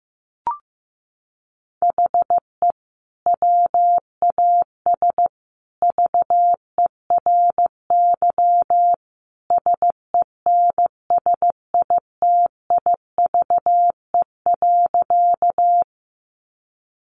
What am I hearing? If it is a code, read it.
Morse code, heard as a keyed tone, about 14 wpm, 700 Hz. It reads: HE WAS VERY SENSITIVE.
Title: Sentences from Top 1400 Words 15wpm